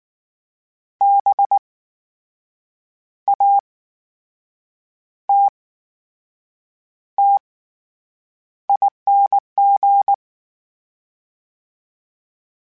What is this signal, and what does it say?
Morse code, heard as a keyed tone, about 19 wpm, spaced as for 7 wpm, 800 Hz. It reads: BATTING
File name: Special INGs / Special INGs-mp3s / VerySlowToFaster Ing @18